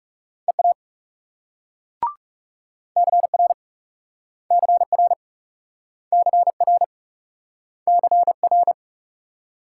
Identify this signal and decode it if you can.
Morse code, heard as a keyed tone, about 42 wpm, 700 Hz. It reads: EA CR CR CR CR